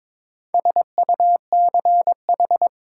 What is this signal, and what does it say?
Morse code, heard as a keyed tone, about 22 wpm, 700 Hz. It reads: SUCH
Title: Sentences from Top 200 Words - New Word Drill - 22wpm